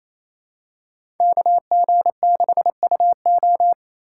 Morse code, heard as a keyed tone, about 28 wpm, 700 Hz. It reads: KG6UO
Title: USA Call Signs - 5 Characters - 2x Character Spacing - 28wpm